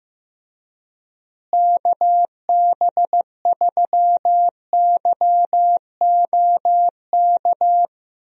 Morse code, heard as a keyed tone, about 15 wpm, 700 Hz. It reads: KB3YOK
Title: USA Call Signs - 6 Characters - Repeated 3x - 15wpm